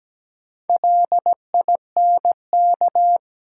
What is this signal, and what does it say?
Morse code, heard as a keyed tone, about 17 wpm, 700 Hz. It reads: LINK
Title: Sentences from Top 1500 Words - New Word Drill Alt - 17wpm